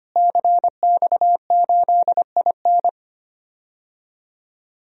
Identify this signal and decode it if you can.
Morse code, heard as a keyed tone, about 25 wpm, 700 Hz. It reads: CX8IN
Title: DXCC-100 Call Signs - 5 Characters - Repeated 2x - 25wpm